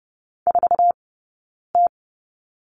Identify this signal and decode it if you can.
Morse code, heard as a keyed tone, about 30 wpm, spaced as for 9 wpm, 690 Hz. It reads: 4T
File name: Lesson 19d - Letter w - Two Characters 30wpm